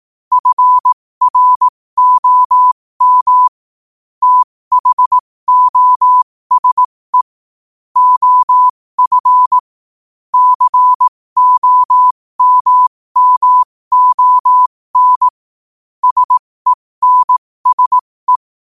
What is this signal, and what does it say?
Morse code, heard as a keyed tone, about 18 wpm, spaced as for 15 wpm, 1000 Hz. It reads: FROM THOSE OF COMMON SENSE
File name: qotdfiles / 15wpm